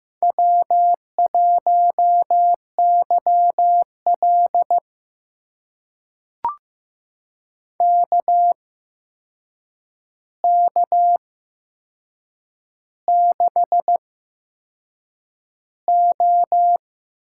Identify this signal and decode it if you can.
Morse code, heard as a keyed tone, about 15 wpm, spaced as for 6 wpm, 700 Hz. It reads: W1YL KK6O
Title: USA Call Signs - 4 Characters - 8x Character Spacing - 15wpm